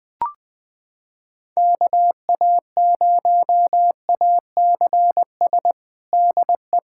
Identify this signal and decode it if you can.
Morse code, heard as a keyed tone, about 21 wpm, 700 Hz. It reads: KA0ACS DE
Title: USA Callsign DE Callsign 20wpm